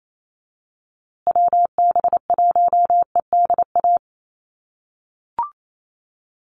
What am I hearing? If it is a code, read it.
Morse code, heard as a keyed tone, about 28 wpm, 700 Hz. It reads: WB1EDA
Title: USA Call Signs Repeated 2x 28wpm